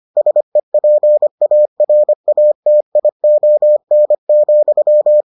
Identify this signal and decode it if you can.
Morse code, heard as a keyed tone, about 25 wpm, 600 Hz. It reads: SEPARATION,